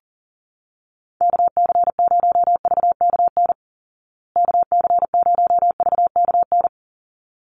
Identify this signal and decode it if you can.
Morse code, heard as a keyed tone, about 40 wpm, 700 Hz. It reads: KC0VKN KC0VKN
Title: Call Signs Repeated 3x - 40wpm